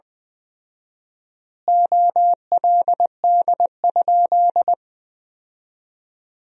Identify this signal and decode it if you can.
Morse code, heard as a keyed tone, about 20 wpm, 700 Hz. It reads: OLD?
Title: 4x Word-Spacing - Sentences from Top 400 Words - 20wpm